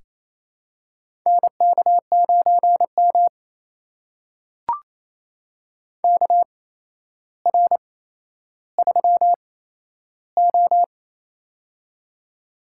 Morse code, heard as a keyed tone, about 28 wpm, spaced as for 12 wpm, 700 Hz. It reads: NK9M KR3O
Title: USA Call Signs - 4 Characters - 8x Character Spacing - 28wpm